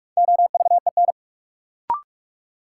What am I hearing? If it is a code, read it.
Morse code, heard as a keyed tone, about 45 wpm, 700 Hz. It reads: OVEN
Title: Sentences from Top 1000 Words - Review Part 4 - New Word Drill - 45wpm